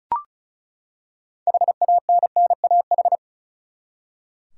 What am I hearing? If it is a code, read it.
Morse code, heard as a keyed tone, about 36 wpm, 700 Hz. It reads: HANNAH
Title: USA Common Names - 35wpm